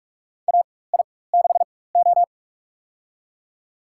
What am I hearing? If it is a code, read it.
Morse code, heard as a keyed tone, about 46 wpm, spaced as for 21 wpm, 700 Hz. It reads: AI6O
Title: USA Call Signs - 4 Characters - 4x Character Spacing - 45wpm